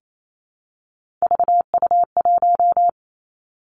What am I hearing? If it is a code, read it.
Morse code, heard as a keyed tone, about 28 wpm, 700 Hz. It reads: VU1
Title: Three Challenging Letter-Number Combinations 28wpm